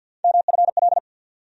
Morse code, heard as a keyed tone, about 52 wpm, 700 Hz. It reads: MFL
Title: USA Call Sign Postfixes - 3 Letters 50wpm